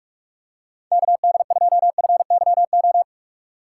Morse code, heard as a keyed tone, about 44 wpm, 700 Hz. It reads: KD2FYO
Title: USA Call Signs - 4x Character Spacing - 45wpm